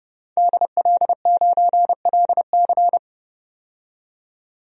Morse code, heard as a keyed tone, about 30 wpm, 700 Hz. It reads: DL9LC
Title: DXCC-100 Call Signs - 5 Characters - 30wpm